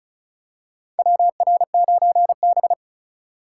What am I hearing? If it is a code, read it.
Morse code, heard as a keyed tone, about 36 wpm, 700 Hz. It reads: WR9B
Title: USA Call Signs - 2x Character Spacing - 35wpm